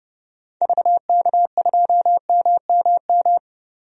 Morse code, heard as a keyed tone, about 31 wpm, 700 Hz. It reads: VK2MMM
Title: DXCC-100 Call Signs - 6 Characters - 30wpm